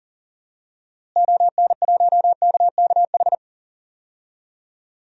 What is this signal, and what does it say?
Morse code, heard as a keyed tone, about 40 wpm, 700 Hz. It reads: ON1KKH
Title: DXCC-100 Call Signs - 6 Characters - Repeated 2x - 40wpm